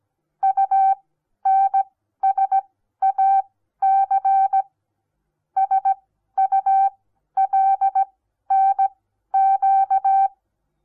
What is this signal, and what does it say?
Morse code, heard as a keyed tone, about 17 wpm, spaced as for 12 wpm, 780 Hz. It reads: UNSAC SULNQ
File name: Onlinelehrgang / morsen / DJ4UF Morsen08